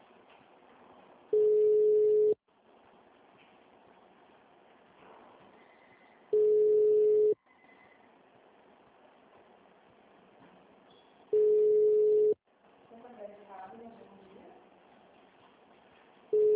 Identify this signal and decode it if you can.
Ringback tone (Europe, 425 Hz), with speech.